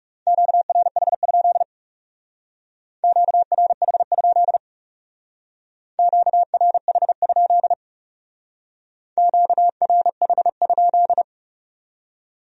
Morse code, heard as a keyed tone, about 40 wpm, 700 Hz. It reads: QRH? QRH? QRH? QRH?